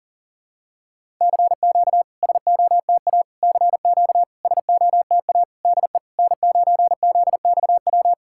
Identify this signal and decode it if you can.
Morse code, heard as a keyed tone, about 40 wpm, 700 Hz. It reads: CQ SOTA CQ SOTA DE N9ZXW
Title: SOTA QSO Practice 40wpm